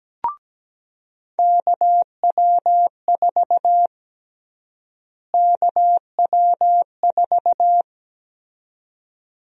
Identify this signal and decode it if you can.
Morse code, heard as a keyed tone, about 17 wpm, 700 Hz. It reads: KW4 KW4
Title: USA Call Sign Prefixes - Repeated 2x - 17wpm